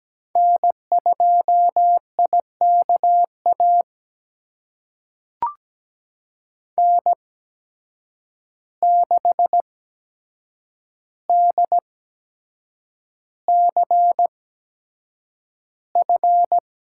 Morse code, heard as a keyed tone, about 17 wpm, spaced as for 7 wpm, 700 Hz. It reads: N2IKA N6DCF
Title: USA Call Signs - 5 Characters - 8x Character Spacing - 17wpm